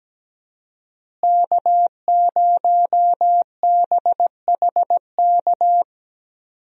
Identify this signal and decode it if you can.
Morse code, heard as a keyed tone, about 17 wpm, 700 Hz. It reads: K0BHK
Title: USA Call Signs - 5 Characters - Repeated 3x - 17wpm